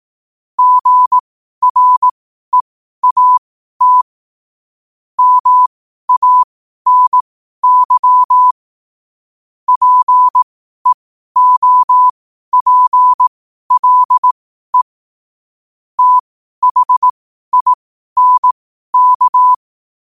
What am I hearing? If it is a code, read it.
Morse code, heard as a keyed tone, about 18 wpm, spaced as for 12 wpm, 1000 Hz. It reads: GREAT MANY PEOPLE THINK
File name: qotdfiles / 12wpm